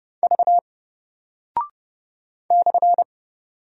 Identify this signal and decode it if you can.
Morse code, heard as a keyed tone, about 30 wpm, 700 Hz. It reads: V /
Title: Single Character (Letter:Number:Punctuation:Prosign) Alt 30wpm